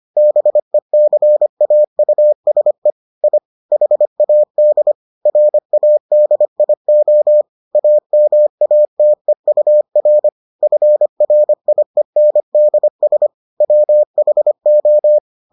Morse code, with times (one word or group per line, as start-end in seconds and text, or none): 0.2-2.9 BECAUSE
3.2-3.4 I
3.7-4.9 HAD
5.3-7.4 RADIO
7.8-10.3 AMATEUR
10.6-13.3 FRIENDS
13.6-15.2 WHO